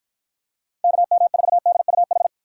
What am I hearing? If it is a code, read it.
Morse code, heard as a keyed tone, about 49 wpm, 700 Hz. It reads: KM4DUH